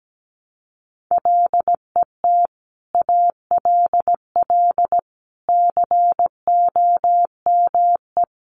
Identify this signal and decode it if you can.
Morse code, heard as a keyed tone, about 17 wpm, 700 Hz. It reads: LET ALL COME
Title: Sentences from Top 1100 Words 17wpm